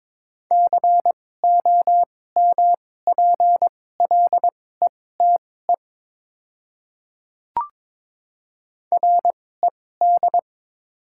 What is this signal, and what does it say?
Morse code, heard as a keyed tone, about 22 wpm, spaced as for 16 wpm, 700 Hz. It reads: COMPLETE RED
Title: 2x Character Spacing - Top 500 Words - 22wpm